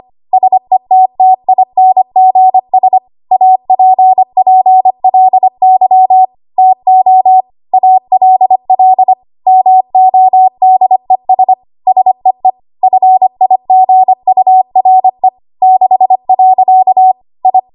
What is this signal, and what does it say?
Morse code, heard as a keyed tone, about 25 wpm, 750 Hz. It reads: SETTINGS APPLY TO ALL MODES SEE FIGURE 6. I
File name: Archive / 25 WPM / 25WPM